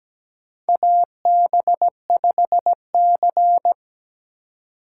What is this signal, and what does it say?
Morse code, heard as a keyed tone, about 17 wpm, 700 Hz. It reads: AB5C
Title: K1USN SST QSO Practice - 17wpm